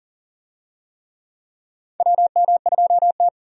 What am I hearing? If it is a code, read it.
Morse code, heard as a keyed tone, about 40 wpm, 700 Hz. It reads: WM2T
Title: USA Call Signs - 4 Characters - 40wpm